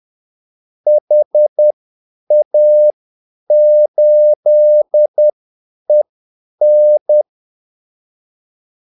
Morse code, heard as a keyed tone, about 10 wpm, spaced as for 8 wpm, 600 Hz. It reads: HA8EN
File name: Callsigns-10-8wpm